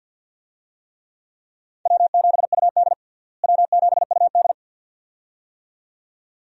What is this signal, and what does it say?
Morse code, heard as a keyed tone, about 51 wpm, 700 Hz. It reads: W7UD W7UD
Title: USA Call Signs - 4 Characters - Repeated 2x - 50wpm